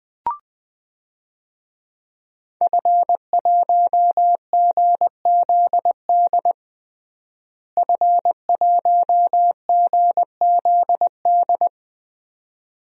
Morse code, heard as a keyed tone, about 20 wpm, 700 Hz. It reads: F1GZD F1GZD
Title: DXCC-100 Call Signs - 5 Characters - Repeated 2x - 20wpm